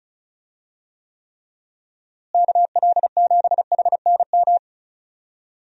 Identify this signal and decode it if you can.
Morse code, heard as a keyed tone, about 36 wpm, 700 Hz. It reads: KL7HNM
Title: USA Call Signs - 6 Characters - Repeated 3x - 35wpm